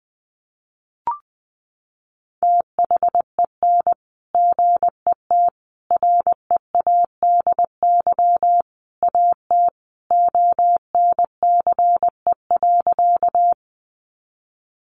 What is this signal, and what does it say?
Morse code, heard as a keyed tone, about 20 wpm, 700 Hz. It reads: THEN GET READY AT ONCE.
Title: Sentences from Top 400 Words 20wpm